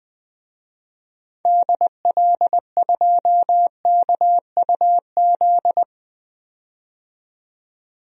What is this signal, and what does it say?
Morse code, heard as a keyed tone, about 20 wpm, 700 Hz. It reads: DL2KUZ
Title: DXCC-100 Call Signs - 6 Characters - 20wpm